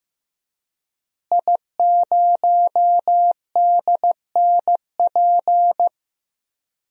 Morse code, heard as a keyed tone, about 15 wpm, 700 Hz. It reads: I0DNP
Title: DXCC-100 Call Signs - 5 Characters - Repeated 3x - 15wpm